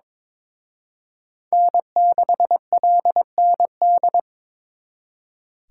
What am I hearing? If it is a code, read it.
Morse code, heard as a keyed tone, about 26 wpm, 700 Hz. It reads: N6LND
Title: Speed Racing - USA 1x3 Call Signs 22wpm